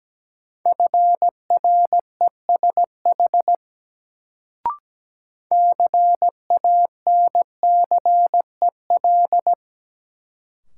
Morse code, heard as a keyed tone, about 17 wpm, 700 Hz. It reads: FRESH CANCEL
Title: Sentences from Top 1200 Words - New Word Drill Alt - 17wpm